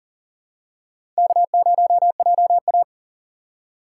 Morse code, heard as a keyed tone, about 40 wpm, 700 Hz. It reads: K0JA
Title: USA Call Signs - 6x Character Spacing - 40wpm